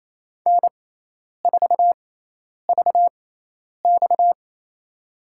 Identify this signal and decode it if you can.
Morse code, heard as a keyed tone, about 28 wpm, spaced as for 10 wpm, 700 Hz. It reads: N4VX
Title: USA Call Signs - 4 Characters - 6x Character Spacing - 28wpm